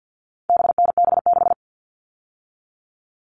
Morse code, heard as a keyed tone, about 50 wpm, 700 Hz. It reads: BNB6
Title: All Cut Numbers - 4-Digit 50wpm